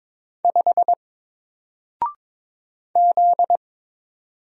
Morse code, heard as a keyed tone, about 22 wpm, 700 Hz. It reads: 5 Z
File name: Single-Letter-Number-22wpm